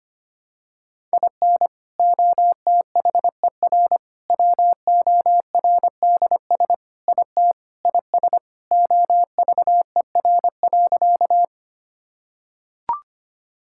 Morse code, heard as a keyed tone, about 25 wpm, 700 Hz. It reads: IN OTHER WORDS IT IS OVER.
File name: practice / 25wpm / Sentences with 3-Word Phrases - 25wpm